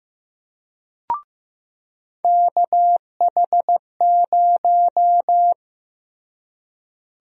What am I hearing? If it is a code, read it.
Morse code, heard as a keyed tone, about 15 wpm, 700 Hz. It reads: KH0
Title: USA Call Sign Prefixes - Repeated 2x - 15wpm